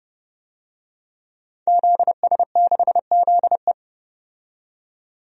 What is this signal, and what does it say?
Morse code, heard as a keyed tone, about 32 wpm, 700 Hz. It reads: ZS6ZE
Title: DXCC-100 Call Signs - 5 Characters - 30wpm